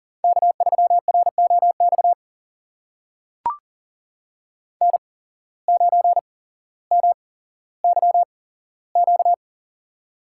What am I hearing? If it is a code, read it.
Morse code, heard as a keyed tone, about 42 wpm, spaced as for 17 wpm, 700 Hz. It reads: K3ROX N9MYQ
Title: USA Call Signs - 5 Characters - 8x Character Spacing - 40wpm